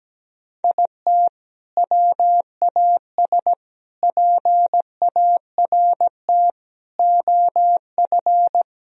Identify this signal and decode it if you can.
Morse code, heard as a keyed tone, about 17 wpm, 700 Hz. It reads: IT WAS PART OF
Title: Sentences from Top 1000 Words - Review Part 4 - 17wpm